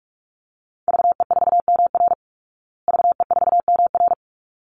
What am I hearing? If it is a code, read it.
Morse code, heard as a keyed tone, about 44 wpm, 700 Hz. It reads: VE4MR VE4MR
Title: Call Signs Repeated 3x - 45wpm